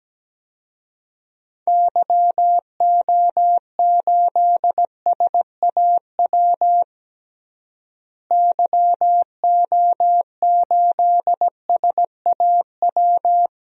Morse code, heard as a keyed tone, about 17 wpm, 700 Hz. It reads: YO8SAW YO8SAW
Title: DXCC-100 Call Signs - 6 Characters - Repeated 3x - 17wpm